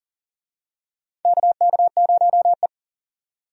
Morse code, heard as a keyed tone, about 40 wpm, 700 Hz. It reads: KK0E
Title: USA Call Signs - 4 Characters - 40wpm